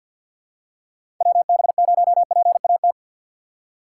Morse code, heard as a keyed tone, about 46 wpm, 700 Hz. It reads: WB0PAT
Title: USA Call Signs - 6 Characters - 6x Character Spacing - 50wpm